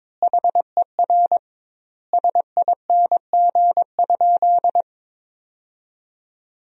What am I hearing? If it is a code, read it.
Morse code, heard as a keyed tone, about 22 wpm, 700 Hz. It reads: HER SING?
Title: 2x Word-Spacing - Sentences from Top 400 Words - 22wpm